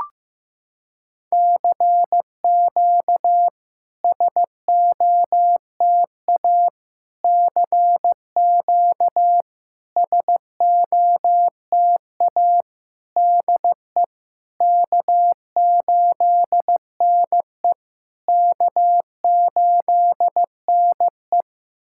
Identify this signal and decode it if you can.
Morse code, heard as a keyed tone, about 15 wpm, 700 Hz. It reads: CQ SOTA CQ SOTA DE K8NE K8NE
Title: SOTA QSO Practice 15wpm